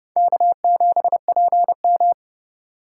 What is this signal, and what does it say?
Morse code, heard as a keyed tone, about 35 wpm, 700 Hz. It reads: K7PM
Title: Speed Racing - USA 1x2 Call Signs 30wpm